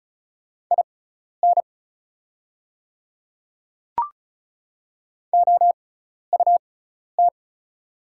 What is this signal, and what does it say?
Morse code, heard as a keyed tone, about 35 wpm, spaced as for 12 wpm, 690 Hz. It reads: IN OUT
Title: 6x Character Spacing - Top 100 Words - 35wpm